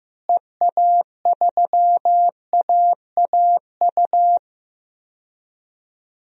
Morse code, heard as a keyed tone, about 15 wpm, 700 Hz. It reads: EA3AAU
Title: DXCC-100 Call Signs - 6 Characters - Repeated 3x - 15wpm